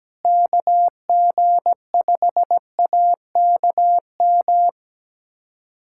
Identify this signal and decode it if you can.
Morse code, heard as a keyed tone, about 17 wpm, 700 Hz. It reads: KG5AKM
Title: USA Callsign DE Callsign - 6 Characters - 17wpm